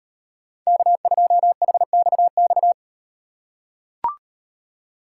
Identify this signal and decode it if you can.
Morse code, heard as a keyed tone, about 38 wpm, 700 Hz. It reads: K2HXX